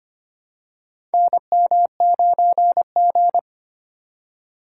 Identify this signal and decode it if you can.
Morse code, heard as a keyed tone, about 25 wpm, 700 Hz. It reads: NM9G
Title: USA Call Signs - 4 Characters - 2x Character Spacing - 25wpm